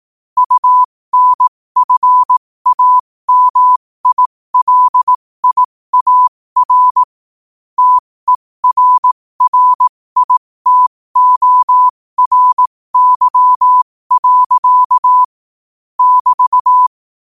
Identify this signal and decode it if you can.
Morse code, heard as a keyed tone, about 18 wpm, spaced as for 15 wpm, 1000 Hz. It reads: UNFAMILIAR TERRITORY. <BT>
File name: 15wpm